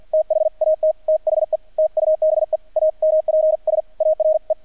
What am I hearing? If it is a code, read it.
Morse code, heard as a keyed tone, about 50 wpm, spaced as for 44 wpm, 630 Hz. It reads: THAT THE TUBE AMPS ARE
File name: archived / CRSnet